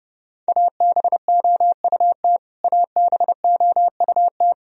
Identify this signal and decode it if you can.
Morse code, heard as a keyed tone, about 31 wpm, 700 Hz. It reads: ABOUT ABOUT
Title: Top 100 Words Repeated 30wpm